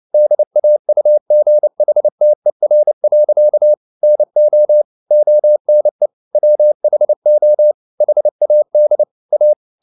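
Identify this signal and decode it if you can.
Morse code, heard as a keyed tone, about 29 wpm, 600 Hz. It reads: DAUGHTER. NO ONE WHO HAD A
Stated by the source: Morse code at 29 wpm, sending Ken Follett's novel Never in English, from CW med Gnister.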